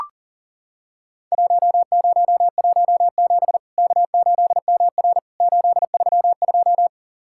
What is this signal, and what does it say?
Morse code, heard as a keyed tone, about 40 wpm, 700 Hz. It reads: 1017 K8MR 832
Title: Worked All Europe DX Contest - QTC Exchange - 40wpm